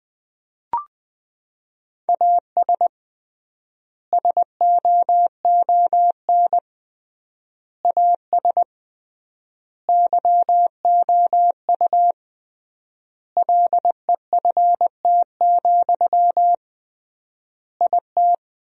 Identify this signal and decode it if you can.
Morse code, heard as a keyed tone, about 20 wpm, 700 Hz. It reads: AS SOON AS YOU LEFT, IT
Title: 3x Word-Spacing - Sentences from Top 400 Words - 20wpm